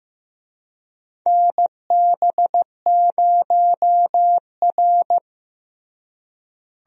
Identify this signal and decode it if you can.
Morse code, heard as a keyed tone, about 15 wpm, 700 Hz. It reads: NB0R
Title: USA Call Signs - 4 Characters - Repeated 2x - 15wpm